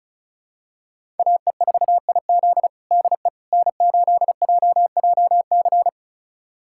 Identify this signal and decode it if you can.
Morse code, heard as a keyed tone, about 36 wpm, 700 Hz. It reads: AE4IZ DE N8JJC
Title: USA Callsign DE Callsign - 5 Characters - 35wpm